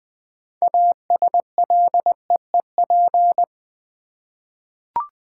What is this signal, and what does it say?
Morse code, heard as a keyed tone, about 20 wpm, 700 Hz. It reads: ASLEEP
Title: 6 Letter Words - First 3 Letters Given - 20wpm